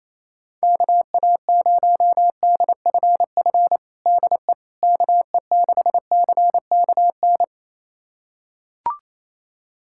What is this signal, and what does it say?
Morse code, heard as a keyed tone, about 28 wpm, 700 Hz. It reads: KA0DFF DE KE6CKN
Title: USA Callsign DE Callsign 28wpm